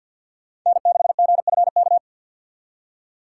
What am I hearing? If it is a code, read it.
Morse code, heard as a keyed tone, about 59 wpm, 700 Hz. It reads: N6GFK
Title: Speed Racing - USA 1x3 Call Signs 50wpm